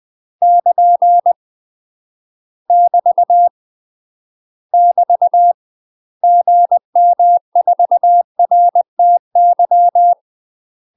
Morse code, heard as a keyed tone, about 20 wpm, 700 Hz. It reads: <KN> <BT> <BT> GM4RTY
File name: UK MASTER QSOs TRAINING / 20WPM / UK MASTER QSOs 0066